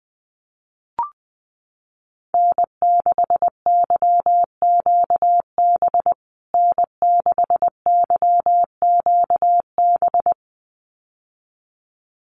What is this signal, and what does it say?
Morse code, heard as a keyed tone, about 20 wpm, 700 Hz. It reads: N6YQB N6YQB
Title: USA Call Signs Repeated 2x 20wpm